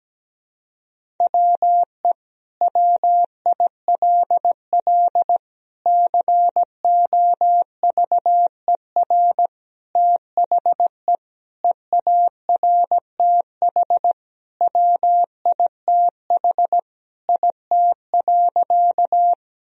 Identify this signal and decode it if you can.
Morse code, heard as a keyed tone, about 17 wpm, 700 Hz. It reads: WE WILL COVER THE EARTH WITH IT.
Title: Sentences from Top 1000 Words - Review Part 1 - 17wpm